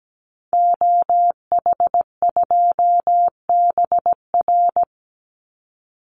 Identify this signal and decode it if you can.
Morse code, heard as a keyed tone, about 17 wpm, 700 Hz. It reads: OH2BR
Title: Call Signs - 17wpm